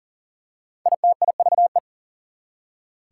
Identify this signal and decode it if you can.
Morse code, heard as a keyed tone, about 40 wpm, 700 Hz. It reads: ITIVE
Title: Prefixes and Suffixes 40wpm